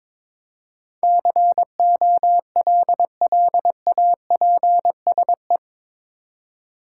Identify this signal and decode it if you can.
Morse code, heard as a keyed tone, about 22 wpm, 700 Hz. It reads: COLLAPSE
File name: practice / 22wpm / Eight Letter Words - 22wpm